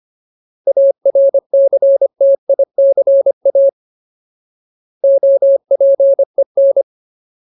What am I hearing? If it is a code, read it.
Morse code, heard as a keyed tone, about 25 wpm, 550 Hz. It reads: ARCTICA OPEN
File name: Ultimate Word List / 25WPM Ultimate